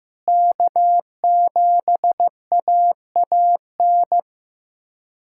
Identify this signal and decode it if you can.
Morse code, heard as a keyed tone, about 16 wpm, 700 Hz. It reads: K7AAN
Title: USA Callsign DE Callsign - 5 Characters - 15wpm